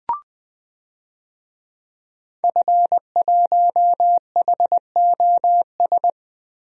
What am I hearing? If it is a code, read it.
Morse code, heard as a keyed tone, about 21 wpm, 700 Hz. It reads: F1HOS
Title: DXCC-100 Call Signs - 5 Characters - Repeated 2x - 20wpm